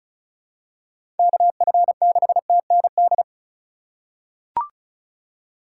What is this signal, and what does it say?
Morse code, heard as a keyed tone, about 36 wpm, 700 Hz. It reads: KF6TND
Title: USA Call Signs - 6 Characters - 4x Character Spacing - 35wpm